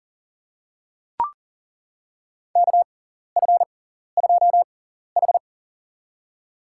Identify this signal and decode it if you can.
Morse code, heard as a keyed tone, about 40 wpm, spaced as for 14 wpm, 700 Hz. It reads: KF2H
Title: USA Call Signs - 6x Character Spacing - 40wpm